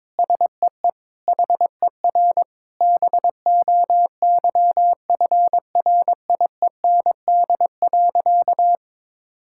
Morse code, heard as a keyed tone, about 22 wpm, 700 Hz. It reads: SEE HER BOYFRIEND.
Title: Sentences from Top 1000 Words - Review Part 6 - 22wpm